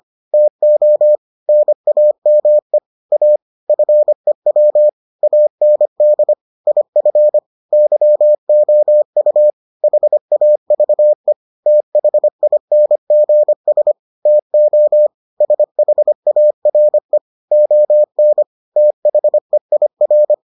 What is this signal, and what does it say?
Morse code, heard as a keyed tone, about 25 wpm, 600 Hz. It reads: TO NAME A FEW AND IF YOU HAVE THINGS TO SHARE ON THEIR